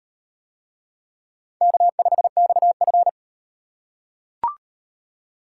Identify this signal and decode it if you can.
Morse code, heard as a keyed tone, about 33 wpm, 700 Hz. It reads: K5XF